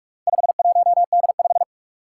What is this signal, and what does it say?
Morse code, heard as a keyed tone, about 46 wpm, 700 Hz. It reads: 51D5
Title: Four Challenging Letter-Number Combinations 45wpm